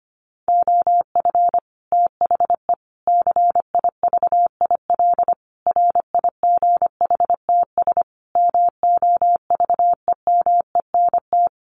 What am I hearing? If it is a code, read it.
Morse code, heard as a keyed tone, about 25 wpm, 700 Hz. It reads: OF THE CIVIL RIGHTS MOVEMENT
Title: Sets of 5 Words Encore2 25wpm